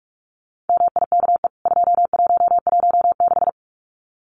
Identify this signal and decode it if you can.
Morse code, heard as a keyed tone, about 44 wpm, 700 Hz. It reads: MIKE 2116
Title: CWT Exchange 45wpm